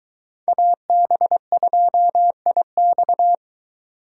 Morse code, heard as a keyed tone, about 22 wpm, 700 Hz. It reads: AB2IX